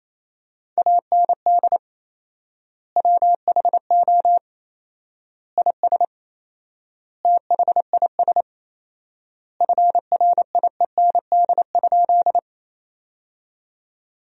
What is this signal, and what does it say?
Morse code, heard as a keyed tone, about 28 wpm, 700 Hz. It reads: AND WHO IS THIS FRIEND?
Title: 4x Word-Spacing - Sentences from Top 300 Words - 28wpm